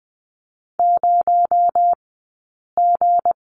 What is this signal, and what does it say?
Morse code, heard as a keyed tone, about 20 wpm, spaced as for 8 wpm, 700 Hz. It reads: 0G
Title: 217 Lesson 35d - Number 0 - Two Characters 20wpm